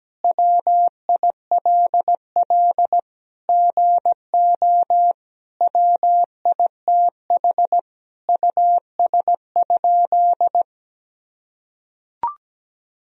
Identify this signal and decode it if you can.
Morse code, heard as a keyed tone, about 17 wpm, 700 Hz. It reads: WILL GO WITH US?
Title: Sentences from Top 1000 Words - Review Part 3 - 17wpm